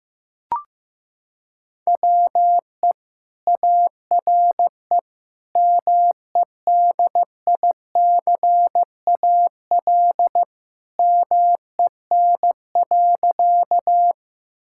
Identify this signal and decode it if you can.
Morse code, heard as a keyed tone, about 15 wpm, 700 Hz. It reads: WE ARE MEDICAL MEN.
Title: Sentences from Top 1000 Words - Review Part 5 - 15wpm